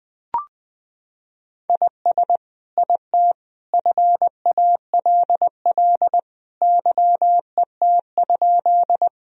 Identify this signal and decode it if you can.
Morse code, heard as a keyed tone, about 20 wpm, 700 Hz. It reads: IS IT FALL YET?
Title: Sentences from Top 1000 Words - Review Part 2 - 20wpm